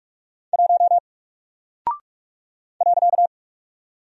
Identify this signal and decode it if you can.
Morse code, heard as a keyed tone, about 44 wpm, 700 Hz. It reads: 1 .
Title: Single Character (Letter:Number:Punctuation:Prosign) Alt 45wpm